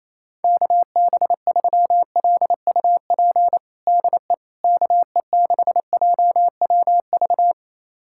Morse code, heard as a keyed tone, about 28 wpm, 700 Hz. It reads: KB3LUP DE KE6JWV
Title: USA Callsign DE Callsign 28wpm